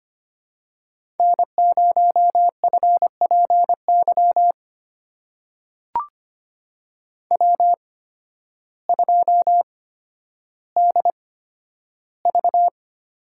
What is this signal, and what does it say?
Morse code, heard as a keyed tone, about 25 wpm, spaced as for 12 wpm, 700 Hz. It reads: N0FPY W2DV
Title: USA Call Signs - 5 Characters - 8x Character Spacing - 25wpm